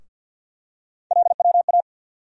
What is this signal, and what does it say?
Morse code, heard as a keyed tone, about 52 wpm, 700 Hz. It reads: LWA
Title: USA Call Sign Postfixes - 3 Letters 50wpm